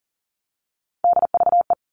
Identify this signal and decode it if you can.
Morse code, heard as a keyed tone, about 40 wpm, 700 Hz. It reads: DVE
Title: All Cut Numbers - 3-Digit 40wpm